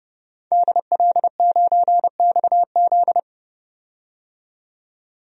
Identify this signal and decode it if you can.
Morse code, heard as a keyed tone, about 30 wpm, 700 Hz. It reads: DL9XZ
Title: DXCC-100 Call Signs - 5 Characters - 30wpm